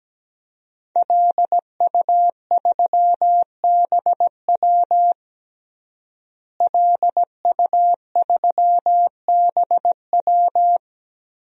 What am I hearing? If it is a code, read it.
Morse code, heard as a keyed tone, about 17 wpm, 700 Hz. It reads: LU3BW LU3BW
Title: DXCC-100 Call Signs - 5 Characters - Repeated 3x - 17wpm